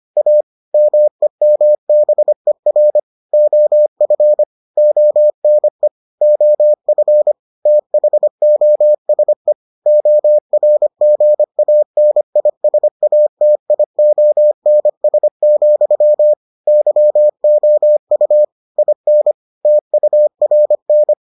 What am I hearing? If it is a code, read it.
Morse code, heard as a keyed tone, about 25 wpm, 600 Hz. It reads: A MEMBER OF ONE OF THOSE ORGANISATIONS, YOU IN TURN